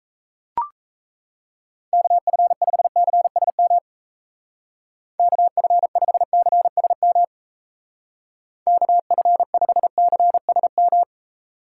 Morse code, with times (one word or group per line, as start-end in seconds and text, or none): 1.9-3.8 KF5CSM
5.2-7.2 KF5CSM
8.7-11.0 KF5CSM